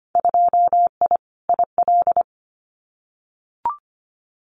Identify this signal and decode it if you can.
Morse code, heard as a keyed tone, about 25 wpm, 700 Hz. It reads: 2I IL
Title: Winter Field Day Exchanges Encore 25wpm